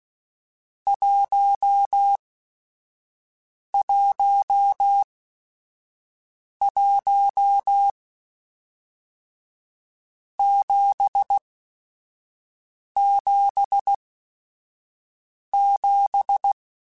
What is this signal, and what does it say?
Morse code, heard as a keyed tone, about 16 wpm, spaced as for 5 wpm, 800 Hz. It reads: 111 777
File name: Groups@5 / Groups@5 mp3s / JZ17@5